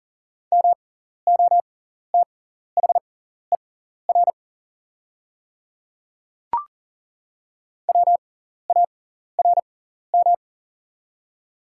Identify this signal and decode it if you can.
Morse code, heard as a keyed tone, about 40 wpm, spaced as for 14 wpm, 690 Hz. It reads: MOTHER WARM
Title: 6x Character Spacing - Top 500 Words - 40wpm